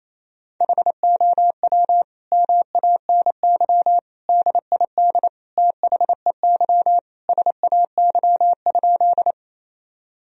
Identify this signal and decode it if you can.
Morse code, heard as a keyed tone, about 28 wpm, 700 Hz. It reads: HOW MANY DID THEY SAY?
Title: Sentences from Top 1000 Words - Review Part 1 - 28wpm